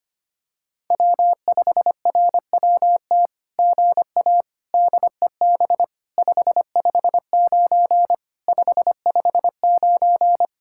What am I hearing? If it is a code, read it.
Morse code, heard as a keyed tone, about 25 wpm, 700 Hz. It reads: W5RWT GA DEB 559 559
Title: SOTA QSO Practice 25wpm